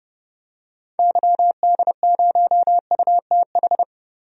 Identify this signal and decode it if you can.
Morse code, heard as a keyed tone, about 30 wpm, 700 Hz. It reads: YD0UTH
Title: DXCC-100 Call Signs - 6 Characters - 30wpm